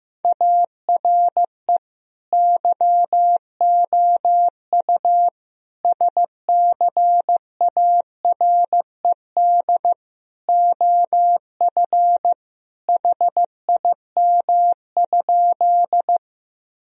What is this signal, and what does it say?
Morse code, heard as a keyed tone, about 15 wpm, 700 Hz. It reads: ARE YOU SCARED OF HIM?
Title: Sentences from Top 1000 Words - Review Part 4 - 15wpm